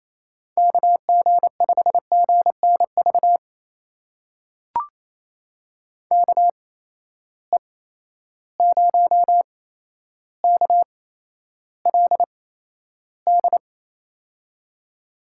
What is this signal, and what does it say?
Morse code, heard as a keyed tone, about 28 wpm, spaced as for 12 wpm, 700 Hz. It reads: KG5GNV KE0KLD
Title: USA Call Signs - 6 Characters - 8x Character Spacing - 28wpm